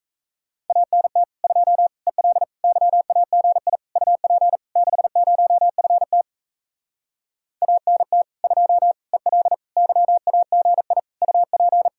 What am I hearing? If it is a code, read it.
Morse code, heard as a keyed tone, about 38 wpm, 700 Hz. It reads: ANT 2 EL YAGI UP 60FT ANT 2 EL YAGI UP